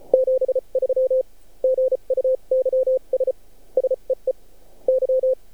Morse code, heard as a keyed tone, about 34 wpm, spaced as for 29 wpm, 520 Hz. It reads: 73 GUYS SEE Y
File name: archived / CRSnet